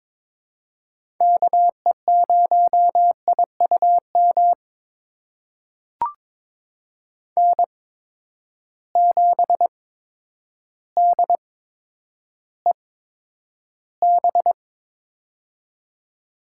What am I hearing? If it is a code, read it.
Morse code, heard as a keyed tone, about 21 wpm, spaced as for 10 wpm, 700 Hz. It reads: KE0IUM N7DEB
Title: USA Call Signs - 8x Character Spacing - 22wpm